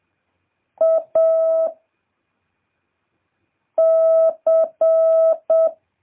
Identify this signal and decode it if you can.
Morse code, heard as a keyed tone, about 7 wpm, 650 Hz. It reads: A C